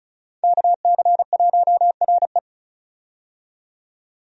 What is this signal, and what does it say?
Morse code, heard as a keyed tone, about 36 wpm, 700 Hz. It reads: KC1RE
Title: USA Call Signs - 5 Characters - 35wpm